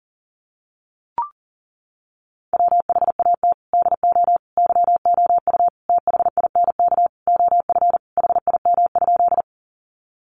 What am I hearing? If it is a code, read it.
Morse code, heard as a keyed tone, about 40 wpm, 700 Hz. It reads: WHAT DO YOU THINK OF HIM?
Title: Sentences from Top 200 Words 40wpm